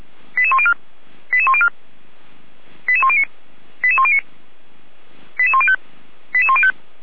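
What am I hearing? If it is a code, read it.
Five-tone selective call (ZVEI), heard as quick runs of five tones, tones of about 70 ms. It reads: 88104 88104 88109 88109 88105 88105
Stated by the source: ZVEI five-tone call received with an SDR.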